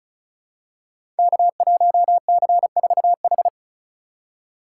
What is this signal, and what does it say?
Morse code, heard as a keyed tone, about 36 wpm, 700 Hz. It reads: K1C4H
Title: Code Groups - 5 Letter:Numbers - 35wpm